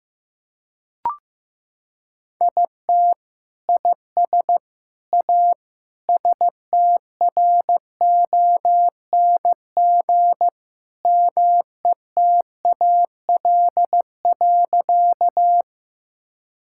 Morse code, heard as a keyed tone, about 15 wpm, 700 Hz. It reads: IT IS A STRONG METAL.
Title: Sentences from Top 1000 Words - Review Part 6 - 15wpm